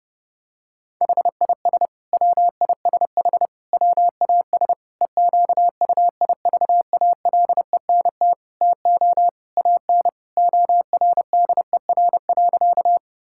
Morse code, heard as a keyed tone, about 30 wpm, 700 Hz. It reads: HIS WISH WAS EQUIVALENT TO AN ORDER.
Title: Sentences from Top 1600 Words 30wpm